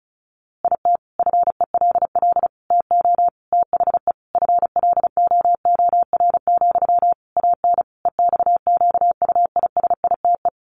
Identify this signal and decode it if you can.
Morse code, heard as a keyed tone, about 34 wpm, 700 Hz. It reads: IT FELL TO THE FLOOR, AN EXQUISITE
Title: A Sound Of Thunder 35wpm